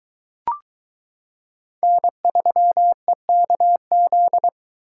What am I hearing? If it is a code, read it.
Morse code, heard as a keyed tone, about 22 wpm, 700 Hz. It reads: N3EKZ